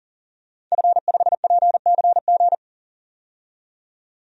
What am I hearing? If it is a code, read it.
Morse code, heard as a keyed tone, about 40 wpm, 700 Hz. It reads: F5PCG
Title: DXCC-100 Call Signs - 5 Characters - 40wpm